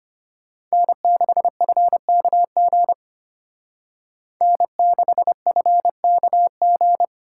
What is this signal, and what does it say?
Morse code, heard as a keyed tone, about 31 wpm, 700 Hz. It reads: N6FKG N6FKG
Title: Speed Racing - USA 1x3 Call Signs 25wpm